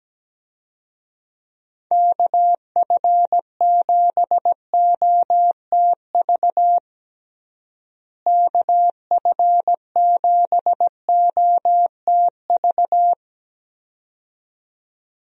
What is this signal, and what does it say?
Morse code, heard as a keyed tone, about 17 wpm, 700 Hz. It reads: KF7OTV KF7OTV
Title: USA Call Signs - 6 Characters - Repeated 2x - 17wpm